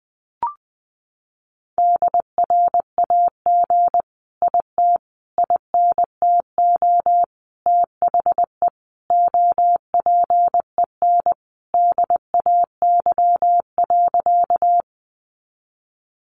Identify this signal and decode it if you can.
Morse code, heard as a keyed tone, about 20 wpm, 700 Hz. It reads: DRAG IT INTO THE OPEN DAY.
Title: Sentences from Top 1500 Words 20wpm